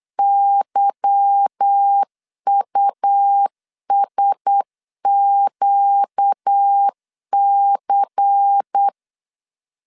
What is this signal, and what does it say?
Morse code, heard as a keyed tone, about 8 wpm, 800 Hz. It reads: YUSQC
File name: LEZIONI / 8 YJ45 / 8 Y 8wpm 3245